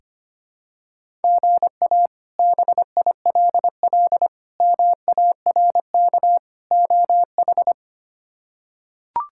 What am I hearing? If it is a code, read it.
Morse code, heard as a keyed tone, about 25 wpm, 700 Hz. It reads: GA BILL MARK OH
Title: K1USN SST QSO Practice - 25wpm